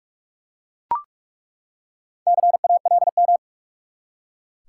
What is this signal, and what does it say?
Morse code, heard as a keyed tone, about 42 wpm, 700 Hz. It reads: CALM
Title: Four Letter Words - 45wpm